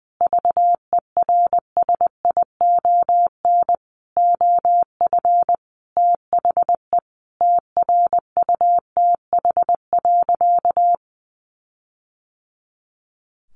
Morse code, heard as a keyed tone, about 20 wpm, 700 Hz. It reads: VERSION OF THE TRUTH.
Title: Sentences from Top 1400 Words 20wpm